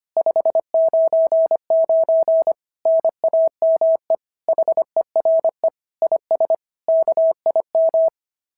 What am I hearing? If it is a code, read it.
Morse code, heard as a keyed tone, about 24 wpm, 650 Hz. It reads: 599 NAME HERE IS KIM
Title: SOTA QSO Practice 25wpm